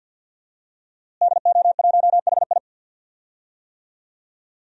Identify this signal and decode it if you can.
Morse code, heard as a keyed tone, about 52 wpm, 700 Hz. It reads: DO1HI
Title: DXCC-100 Call Signs - 5 Characters - Repeated 2x - 50wpm